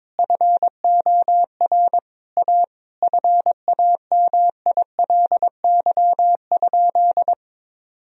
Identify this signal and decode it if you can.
Morse code, heard as a keyed tone, about 22 wpm, 700 Hz. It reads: FOR A FAMILY?
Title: Sentences from Top 1000 Words - Review Part 4 - 22wpm